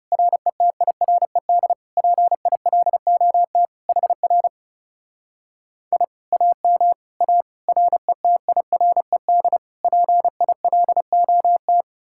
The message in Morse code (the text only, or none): RETIRED PILOT HR I AM A RETIRED PILOT